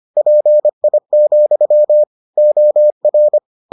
Morse code, heard as a keyed tone, about 25 wpm, 600 Hz. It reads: PI, OR